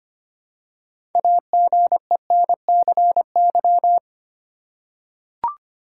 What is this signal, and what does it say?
Morse code, heard as a keyed tone, about 25 wpm, 700 Hz. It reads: AGENCY
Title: Six Letter Words - 25wpm